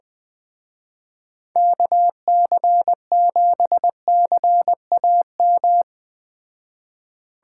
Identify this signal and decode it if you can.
Morse code, heard as a keyed tone, about 20 wpm, 700 Hz. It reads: KC7CAM